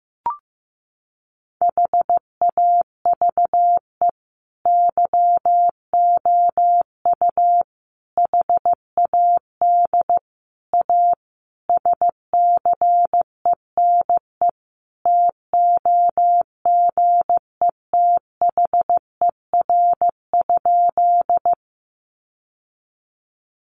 Morse code, heard as a keyed tone, about 14 wpm, 700 Hz. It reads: HAVE YOU HAD A SCENE TOGETHER?
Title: Sentences from Top 1100 Words 15wpm